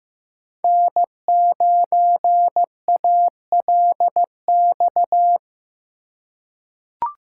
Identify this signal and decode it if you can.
Morse code, heard as a keyed tone, about 15 wpm, 700 Hz. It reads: N9ALX
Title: USA Call Signs - 5 Characters - 2x Character Spacing - 15wpm